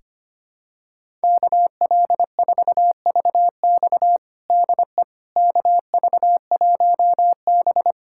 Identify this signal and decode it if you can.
Morse code, heard as a keyed tone, about 25 wpm, 700 Hz. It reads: KL4VX DE KV1B
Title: USA Callsign DE Callsign 25wpm